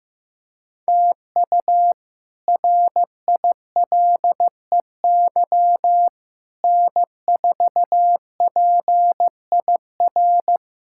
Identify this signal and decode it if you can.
Morse code, heard as a keyed tone, about 15 wpm, 700 Hz. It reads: TU RILEY N4PIR
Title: K1USN SST QSO Practice - 15wpm